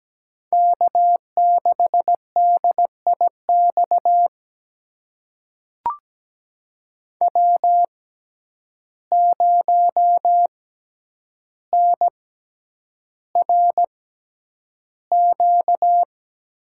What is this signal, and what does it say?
Morse code, heard as a keyed tone, about 17 wpm, spaced as for 9 wpm, 700 Hz. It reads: K6DIX W0NRQ
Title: USA Call Signs - 6x Character Spacing - 17wpm